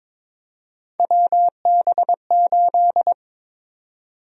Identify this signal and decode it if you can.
Morse code, heard as a keyed tone, about 23 wpm, 700 Hz. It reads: WB8
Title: USA Call Sign Prefixes - Repeated 2x - 22wpm